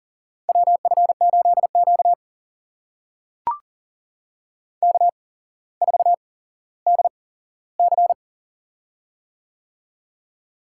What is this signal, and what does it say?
Morse code, heard as a keyed tone, about 40 wpm, spaced as for 17 wpm, 700 Hz. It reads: WF8Q K4DC
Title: USA Call Signs - 4 Characters - 8x Character Spacing - 40wpm